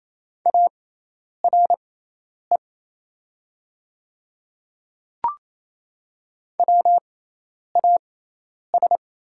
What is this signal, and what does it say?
Morse code, heard as a keyed tone, about 28 wpm, spaced as for 10 wpm, 700 Hz. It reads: ARE WAS
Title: 6x Character Spacing - Top 100 Words - 28wpm